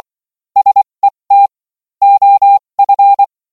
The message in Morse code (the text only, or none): SET OF